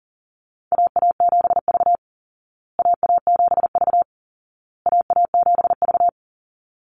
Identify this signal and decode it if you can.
Morse code, heard as a keyed tone, about 40 wpm, 700 Hz. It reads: AA7V AA7V AA7V
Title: Call Signs Repeated 2x - 40wpm